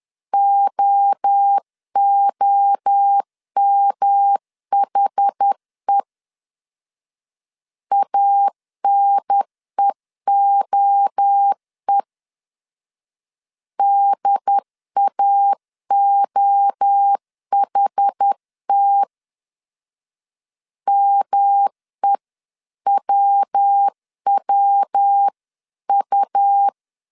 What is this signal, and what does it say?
Morse code, heard as a keyed tone, about 10 wpm, 800 Hz. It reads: OOMHE ANEOE DAOHT MEWWU
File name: 10wpm